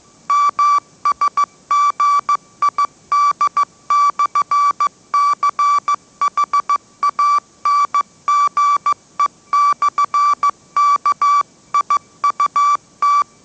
Morse code, heard as a keyed tone, about 15 wpm, 1160 Hz. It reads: MSGID/CHANGE/KIUT
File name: mrx / mrx066 16wpm